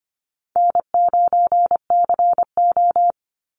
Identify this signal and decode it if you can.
Morse code, heard as a keyed tone, about 25 wpm, 700 Hz. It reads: N9CO
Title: Easy Call Signs Repeated 3x - 25wpm